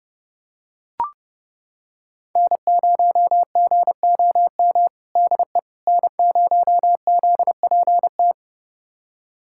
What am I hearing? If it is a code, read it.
Morse code, heard as a keyed tone, about 30 wpm, 700 Hz. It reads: N0GOM DE N0ZPT
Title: USA Callsign DE Callsign - 5 Characters - 30wpm